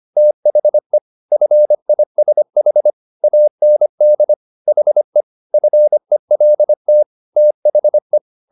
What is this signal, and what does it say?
Morse code, heard as a keyed tone, about 25 wpm, 600 Hz. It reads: THE FISH AND HE FELT THE
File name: the old man and the sea part 1 25wpm